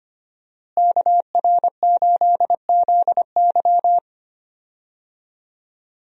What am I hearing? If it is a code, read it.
Morse code, heard as a keyed tone, about 25 wpm, 700 Hz. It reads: KR8ZY
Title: USA Call Signs - 5 Characters - Repeated 3x - 25wpm